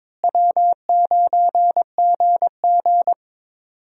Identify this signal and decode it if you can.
Morse code, heard as a keyed tone, about 22 wpm, 700 Hz. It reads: W9GG
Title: USA Call Signs - 4 Characters - Repeated 3x - 22wpm